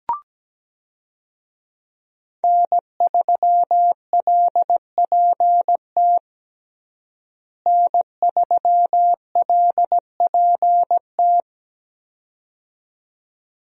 Morse code, heard as a keyed tone, about 17 wpm, 700 Hz. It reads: N3LPT N3LPT
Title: USA Call Signs - 5 Characters - Repeated 2x - 17wpm